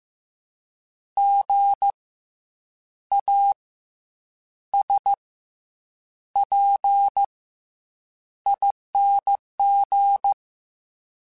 Morse code, heard as a keyed tone, about 15 wpm, 800 Hz. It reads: G A S P ING
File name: Special INGs / Special INGs-mp3s / SlowToFaster ing @13